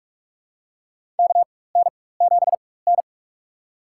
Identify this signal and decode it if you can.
Morse code, heard as a keyed tone, about 42 wpm, spaced as for 21 wpm, 700 Hz. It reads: KN7N
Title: USA Call Signs - 4x Character Spacing - 45wpm